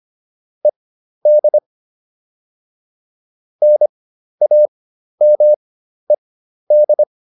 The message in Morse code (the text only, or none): ED NAMED